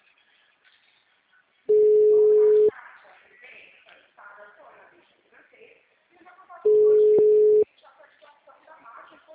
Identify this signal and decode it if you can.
Ringback tone (Europe, 425 Hz), with speech.